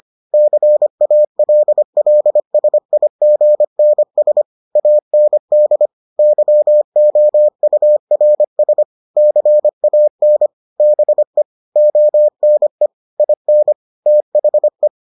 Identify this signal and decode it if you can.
Morse code, heard as a keyed tone, about 25 wpm, 600 Hz. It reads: CALLSIGNS AND YOURS CAN BE ONE IN THE